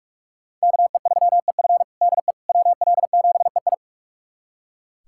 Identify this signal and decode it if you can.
Morse code, heard as a keyed tone, about 42 wpm, 700 Hz. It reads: KE3EF DE WL7EI
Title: USA Callsign DE Callsign - 5 Characters - 45wpm